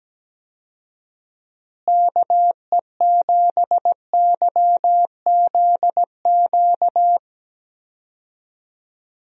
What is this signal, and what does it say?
Morse code, heard as a keyed tone, about 17 wpm, 700 Hz. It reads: KE7YZQ
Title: USA Call Signs - 6 Characters - Repeated 2x - 17wpm